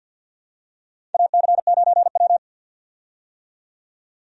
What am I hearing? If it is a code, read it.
Morse code, heard as a keyed tone, about 54 wpm, 700 Hz. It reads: AC9W
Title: USA Call Signs - 4 Characters - 50wpm